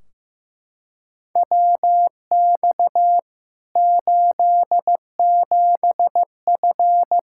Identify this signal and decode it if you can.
Morse code, heard as a keyed tone, about 15 wpm, 700 Hz. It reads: WX 87F
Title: Practice QSO - Exchange 2 - Info - 15wpm